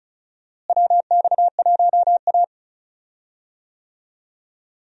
Morse code, heard as a keyed tone, about 36 wpm, 700 Hz. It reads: WX1A
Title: USA Call Signs - 4 Characters - Repeated 2x - 35wpm